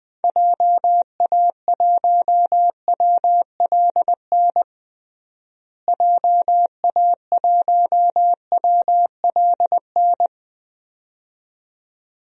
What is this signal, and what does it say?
Morse code, heard as a keyed tone, about 20 wpm, 700 Hz. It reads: JA1WLN JA1WLN
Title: DXCC-100 Call Signs - 6 Characters - Repeated 2x - 20wpm